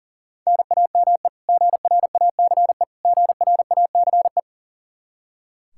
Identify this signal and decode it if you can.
Morse code, heard as a keyed tone, about 37 wpm, 700 Hz. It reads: NAME GRACE GRACE
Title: Practice QSO - Exchange 1 - Beginning - 40wpm